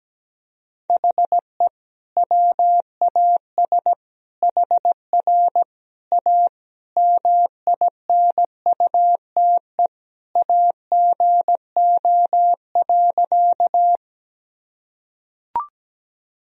Morse code, heard as a keyed tone, about 17 wpm, 700 Hz. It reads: HE WAS HR A MINUTE AGO.
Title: Sentences from Top 500 Words and Abbreviations 17wpm